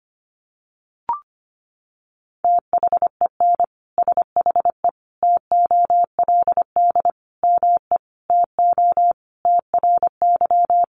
Morse code, heard as a keyed tone, about 25 wpm, 700 Hz. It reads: THEN SHE TOLD ME TO TRY
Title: Sentences from Top 400 Words 25wpm